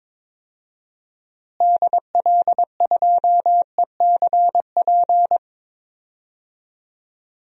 Morse code, heard as a keyed tone, about 22 wpm, 700 Hz. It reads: DL2ECP
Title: DXCC-100 Call Signs - 6 Characters - Repeated 3x - 22wpm